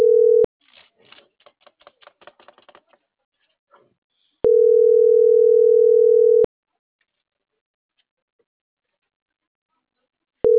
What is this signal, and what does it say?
Ringback tone (North America), 440 + 480 Hz.
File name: CENTRAL MAYAB / out esmeralda.garcia